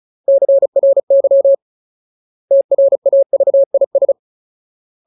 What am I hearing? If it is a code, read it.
Morse code, heard as a keyed tone, about 34 wpm, 550 Hz. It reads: CRY TRAVIS
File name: Ultimate Word List / 35WPM Ultimate